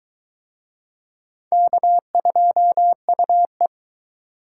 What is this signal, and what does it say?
Morse code, heard as a keyed tone, about 22 wpm, 700 Hz. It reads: K2UE